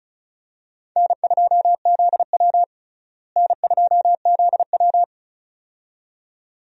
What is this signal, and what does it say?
Morse code, heard as a keyed tone, about 36 wpm, 700 Hz. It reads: N2ZW N2ZW
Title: USA Call Signs - 4 Characters - Repeated 2x - 35wpm